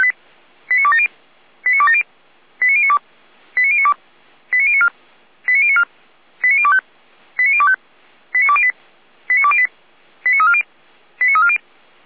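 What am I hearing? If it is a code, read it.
Five-tone selective call (ZVEI), heard as quick runs of five tones, tones of about 70 ms. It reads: 89270 89270 89092 89092 89094 89094 89025 89025 89208 89208 89340 89340
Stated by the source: ZVEI five-tone selective call received by SDR.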